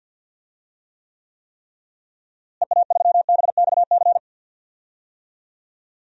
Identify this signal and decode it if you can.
Morse code, heard as a keyed tone, about 51 wpm, 700 Hz. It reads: EA3BXC
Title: DXCC-100 Call Signs - 6 Characters - Repeated 3x - 50wpm